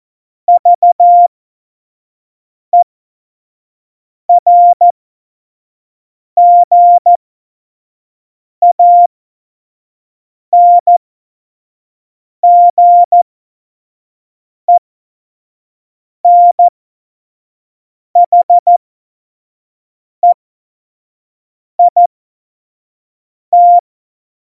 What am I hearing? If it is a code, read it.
Morse code, heard as a keyed tone, about 14 wpm, spaced as for 5 wpm, 700 Hz. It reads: VERGANGENHEIT
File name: Morsekurs / Witze / Witz 23. Juli 2019, 25 F70